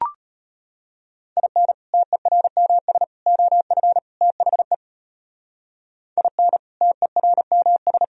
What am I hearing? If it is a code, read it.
Morse code, heard as a keyed tone, about 38 wpm, 700 Hz. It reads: IN TERMS OF THE IN TERMS